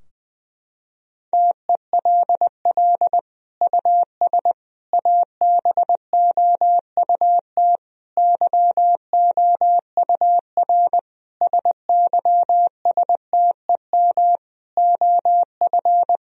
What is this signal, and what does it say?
Morse code, heard as a keyed tone, about 20 wpm, 700 Hz. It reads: TELL US ABOUT YOUR SYSTEM OF
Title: Sentences from Top 1000 Words - Review Part 3 - 20wpm